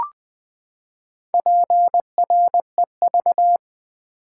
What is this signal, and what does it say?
Morse code, heard as a keyed tone, about 20 wpm, 700 Hz. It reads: PREV
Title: Build A Word - 4 Letters At A Time - 10-Letter Words - 20wpm